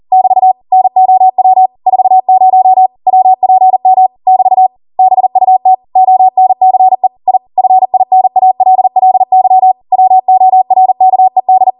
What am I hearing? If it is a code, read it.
Morse code, heard as a keyed tone, about 40 wpm, 750 Hz. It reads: <BT> NOW 40 WPM <BT> BUT ONCE I FINALLY WORKED